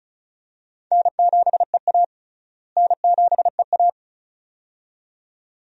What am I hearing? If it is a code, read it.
Morse code, heard as a keyed tone, about 34 wpm, 700 Hz. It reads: N7EA N7EA
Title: USA Call Signs - 4 Characters - Repeated 2x - 35wpm